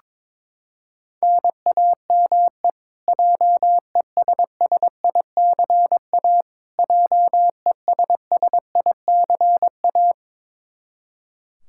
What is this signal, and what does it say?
Morse code, heard as a keyed tone, about 22 wpm, 700 Hz. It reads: NAME JESSICA JESSICA
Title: Practice QSO - Exchange 1 - Beginning - 22wpm